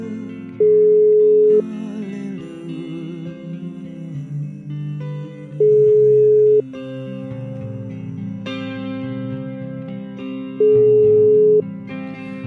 Ringback tone (Europe, 425 Hz).